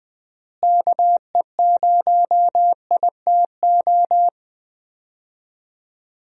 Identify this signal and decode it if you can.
Morse code, heard as a keyed tone, about 20 wpm, 700 Hz. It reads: KE0ITO
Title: USA Call Signs - 6 Characters - Repeated 2x - 20wpm